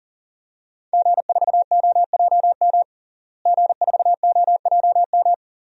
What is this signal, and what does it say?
Morse code, heard as a keyed tone, about 40 wpm, 700 Hz. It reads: G4OJM G4OJM
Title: DXCC-100 Call Signs - 5 Characters - Repeated 3x - 40wpm